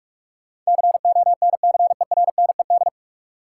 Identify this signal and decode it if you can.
Morse code, heard as a keyed tone, about 46 wpm, 700 Hz. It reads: CONCERNED
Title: Sentences from Top 1000 Words - Review Part 6 - New Word Drill - 45wpm